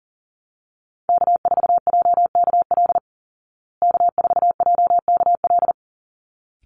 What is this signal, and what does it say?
Morse code, heard as a keyed tone, about 40 wpm, 700 Hz. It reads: K4JKL K4JKL
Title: Call Signs Repeated 3x - 40wpm